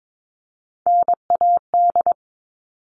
Morse code, heard as a keyed tone, about 22 wpm, 700 Hz. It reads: NAD
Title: All Cut Numbers - 3-Digit 22wpm